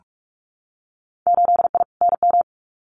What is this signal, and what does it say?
Morse code, heard as a keyed tone, about 46 wpm, 700 Hz. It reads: 8I NM
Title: Winter Field Day Exchanges Encore 45wpm